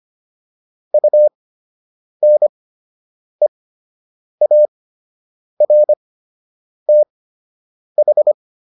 Morse code, heard as a keyed tone, about 25 wpm, spaced as for 8 wpm, 600 Hz. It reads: UNEARTH